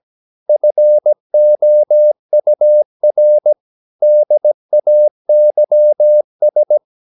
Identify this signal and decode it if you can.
Morse code, heard as a keyed tone, about 17 wpm, 600 Hz. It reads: FOUR DAYS
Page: Romanen Never af Ken Follet på engelsk.